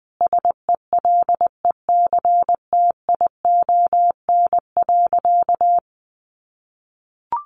Morse code, heard as a keyed tone, about 20 wpm, 700 Hz. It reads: SELECTION.
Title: Sentences from Top 1400 Words 20wpm